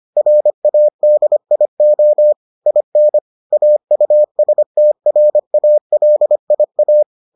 Morse code, heard as a keyed tone, about 25 wpm, 600 Hz. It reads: RADIO IN AUSTRALIA